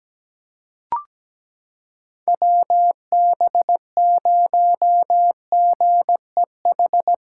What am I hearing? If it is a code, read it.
Morse code, heard as a keyed tone, about 17 wpm, 700 Hz. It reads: WB0GEH
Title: USA Callsign DE Callsign 17wpm